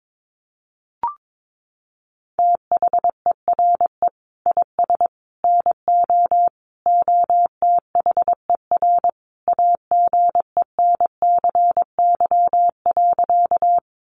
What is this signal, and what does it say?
Morse code, heard as a keyed tone, about 22 wpm, 700 Hz. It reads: THERE IS NO OTHER AGENCY.
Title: Sentences from Top 1300 Words 22wpm